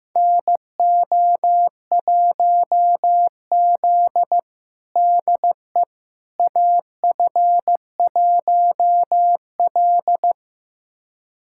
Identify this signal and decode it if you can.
Morse code, heard as a keyed tone, about 15 wpm, 700 Hz. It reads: NO1Z DE AF1L
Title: USA Callsign DE Callsign - 4 Characters - 15wpm